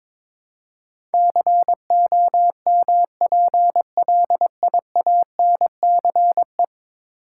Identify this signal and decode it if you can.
Morse code, heard as a keyed tone, about 23 wpm, 700 Hz. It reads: COMPLIANCE
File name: practice / 22wpm / 10 Letter Words - First 3 Letters Given - 22wpm